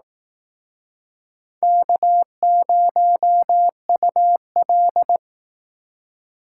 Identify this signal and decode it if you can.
Morse code, heard as a keyed tone, about 19 wpm, 700 Hz. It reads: K0UL
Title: Speed Racing - USA 1x2 Call Signs 15wpm